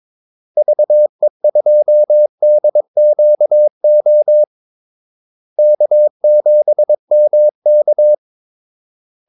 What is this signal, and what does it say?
Morse code, heard as a keyed tone, about 22 wpm, 600 Hz. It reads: VE2DQO K7MK
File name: Callsigns-22wpm